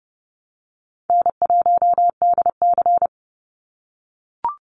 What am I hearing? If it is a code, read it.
Morse code, heard as a keyed tone, about 30 wpm, 700 Hz. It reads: N1DC
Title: Easy Call Signs - 30wpm